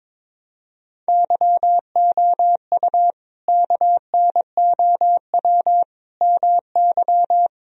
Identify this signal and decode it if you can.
Morse code, heard as a keyed tone, about 22 wpm, 700 Hz. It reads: YOU KNOW MY
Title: Sentences from Top 1000 Words - Review Part 3 - 22wpm